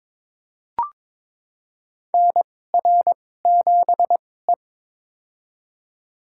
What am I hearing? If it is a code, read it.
Morse code, heard as a keyed tone, about 22 wpm, spaced as for 16 wpm, 700 Hz. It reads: NR7E
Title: USA Call Signs - 4 Characters - 2x Character Spacing - 22wpm